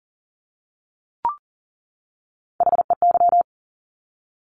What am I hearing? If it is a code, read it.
Morse code, heard as a keyed tone, about 40 wpm, 700 Hz. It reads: HEY
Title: Top Three Letter Combinations in Words 40wpm